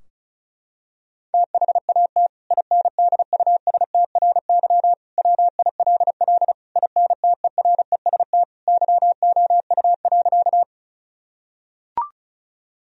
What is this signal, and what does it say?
Morse code, heard as a keyed tone, about 36 wpm, 700 Hz. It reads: THAT INDUSTRY WILL INTEREST YOU.
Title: Sentences from Top 1000 Words - Review Part 4 - 35wpm